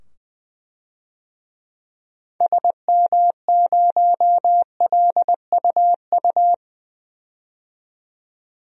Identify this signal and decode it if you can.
Morse code, heard as a keyed tone, about 20 wpm, 700 Hz. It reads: SM0LUU
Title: DXCC-100 Call Signs - 6 Characters - 20wpm